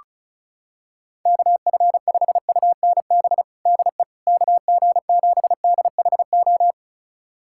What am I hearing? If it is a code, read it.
Morse code, heard as a keyed tone, about 36 wpm, 700 Hz. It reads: KF5UNB DE KG7DHO
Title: USA Callsign DE Callsign - 6 Characters - 35wpm